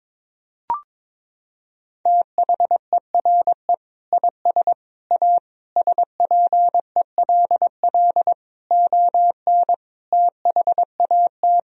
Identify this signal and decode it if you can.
Morse code, heard as a keyed tone, about 22 wpm, 700 Hz. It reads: THERE IS A SPELL ON THAT
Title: Sentences from Top 1000 Words - Review Part 1 - 22wpm